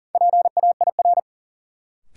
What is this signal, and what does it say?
Morse code, heard as a keyed tone, about 40 wpm, 700 Hz. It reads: PAIR
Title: Sentences from Top 1200 Words - New Word Drill - 40wpm